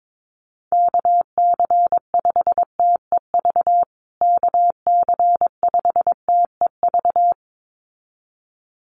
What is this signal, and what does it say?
Morse code, heard as a keyed tone, about 22 wpm, 700 Hz. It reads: KC5TEV KC5TEV
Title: USA Call Signs Repeated 2x 22wpm